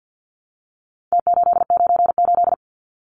Morse code, heard as a keyed tone, about 50 wpm, 700 Hz. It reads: T898
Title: Numbers - Contest Style - 4-Digit 50wpm